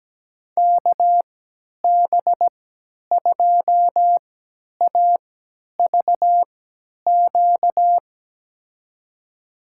Morse code, heard as a keyed tone, about 17 wpm, spaced as for 10 wpm, 700 Hz. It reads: KB2AVQ
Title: USA Call Signs - 6 Characters - 3x Character Spacing - 17wpm